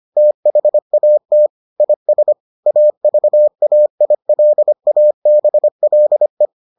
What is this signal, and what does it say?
Morse code, heard as a keyed tone, about 25 wpm, 600 Hz. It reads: THAT IS AVAILABLE